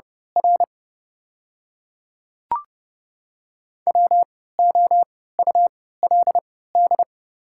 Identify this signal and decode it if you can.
Morse code, heard as a keyed tone, about 30 wpm, spaced as for 17 wpm, 690 Hz. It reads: R WOULD
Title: 3x Character Spacing - Top 300 Words - 30wpm